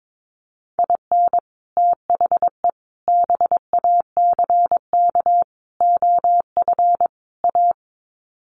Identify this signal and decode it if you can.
Morse code, heard as a keyed tone, about 22 wpm, 700 Hz. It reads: IN THE BACK OF A
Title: Sets of 5 Words Encore2 22wpm